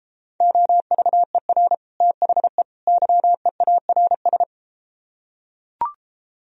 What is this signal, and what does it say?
Morse code, heard as a keyed tone, about 33 wpm, 700 Hz. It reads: OVER THE YEARS